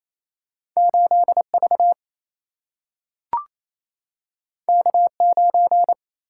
Morse code, heard as a keyed tone, about 28 wpm, 700 Hz. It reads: 8 V K 9
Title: Single-Letter Single-Number 28wpm